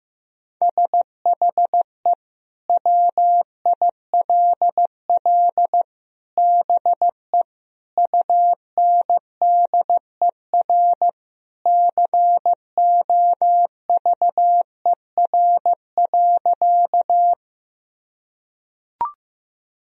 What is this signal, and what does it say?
Morse code, heard as a keyed tone, about 15 wpm, 700 Hz. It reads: SHE WILL BE UNDER COVER.
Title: Sentences from Top 1000 Words - Review Part 1 - 15wpm